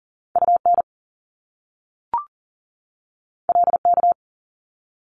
Morse code, heard as a keyed tone, about 39 wpm, 700 Hz. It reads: UN LK
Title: Top Two Letter Combinations in Words 40wpm